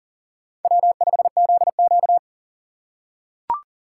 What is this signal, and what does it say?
Morse code, heard as a keyed tone, about 40 wpm, 700 Hz. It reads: W5ZQ
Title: USA Call Signs - 4 Characters - 6x Character Spacing - 40wpm